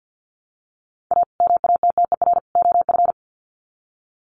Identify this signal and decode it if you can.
Morse code, heard as a keyed tone, about 48 wpm, 700 Hz. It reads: A MATTER OF
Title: Sets of 3 Words 50wpm